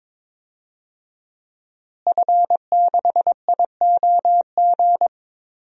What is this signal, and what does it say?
Morse code, heard as a keyed tone, about 22 wpm, 700 Hz. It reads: F6IOG
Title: DXCC-100 Call Signs - 5 Characters - Repeated 2x - 22wpm